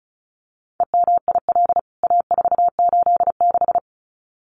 Morse code, heard as a keyed tone, about 34 wpm, 700 Hz. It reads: EMIL A486
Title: CWT Exchange 35wpm